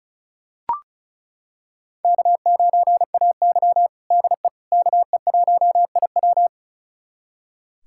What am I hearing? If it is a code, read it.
Morse code, heard as a keyed tone, about 36 wpm, 700 Hz. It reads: K9AY DE KE1IW
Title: USA Callsign DE Callsign 35wpm